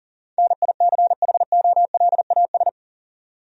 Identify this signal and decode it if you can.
Morse code, heard as a keyed tone, about 40 wpm, 700 Hz. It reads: NICHOLAS
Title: USA Common Names - 40wpm